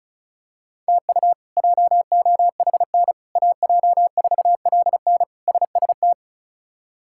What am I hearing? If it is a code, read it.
Morse code, heard as a keyed tone, about 34 wpm, 700 Hz. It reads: TU JOHN AJ4LN SST
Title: K1USN SST QSO Practice - 35wpm